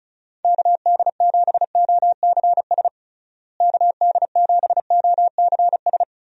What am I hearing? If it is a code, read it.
Morse code, heard as a keyed tone, about 34 wpm, 700 Hz. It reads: KD7OCS KD7OCS
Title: USA Call Signs - 6 Characters - Repeated 3x - 35wpm